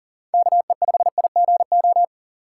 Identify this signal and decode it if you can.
Morse code, heard as a keyed tone, about 40 wpm, 700 Hz. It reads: KE5IGO
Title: USA Call Signs - 6 Characters - 2x Character Spacing - 40wpm